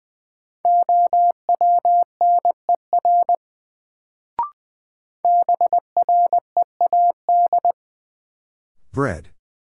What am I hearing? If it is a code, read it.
Morse code, heard as a keyed tone, about 21 wpm, 700 Hz. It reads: OWNER BREAD
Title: Sentences from Top 1600 Words - New Word Drill - 20wpm